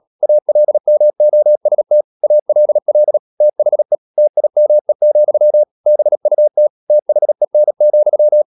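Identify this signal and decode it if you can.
Morse code, heard as a keyed tone, about 37 wpm, 600 Hz. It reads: ALMOST ALL THE TIME, BUT THEN,
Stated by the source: Morse code practice at 35–39 wpm sending English text of Ken Follett's novel Never, from CW med Gnister.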